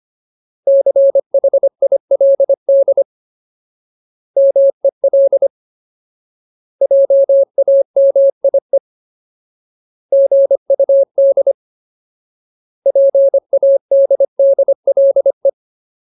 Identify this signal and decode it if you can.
Morse code, heard as a keyed tone, about 25 wpm, 550 Hz. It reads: CHILD MEL JAMIE GUD PADDLE
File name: Ultimate Word List / 25WPM Ultimate